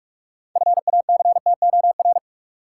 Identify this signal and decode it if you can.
Morse code, heard as a keyed tone, about 46 wpm, 700 Hz. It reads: FACTOR
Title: Sentences from Top 1300 Words - New Word Drill - 45wpm